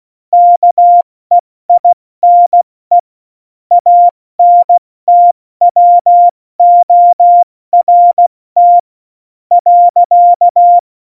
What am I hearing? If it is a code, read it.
Morse code, heard as a keyed tone, about 16 wpm, 700 Hz. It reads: KEINE ANTWORT .
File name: Morsekurs / Witze / Witz 14. August 2018, 70 F80